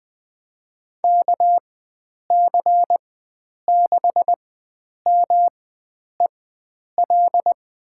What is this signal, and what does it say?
Morse code, heard as a keyed tone, about 20 wpm, spaced as for 9 wpm, 700 Hz. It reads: KC6MEL
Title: USA Call Signs - 6 Characters - 4x Character Spacing - 20wpm